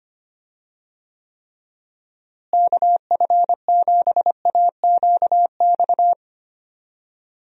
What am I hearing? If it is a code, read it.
Morse code, heard as a keyed tone, about 25 wpm, 700 Hz. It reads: KF7AQX
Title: USA Call Signs - 6 Characters - Repeated 3x - 25wpm